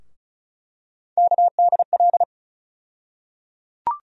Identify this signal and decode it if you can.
Morse code, heard as a keyed tone, about 36 wpm, 700 Hz. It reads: KDL
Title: USA Call Sign Postfixes - 3 Letters 35wpm